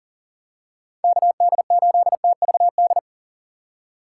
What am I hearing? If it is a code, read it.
Morse code, heard as a keyed tone, about 40 wpm, 700 Hz. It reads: KD8TVD
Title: USA Call Signs - 6 Characters - 6x Character Spacing - 40wpm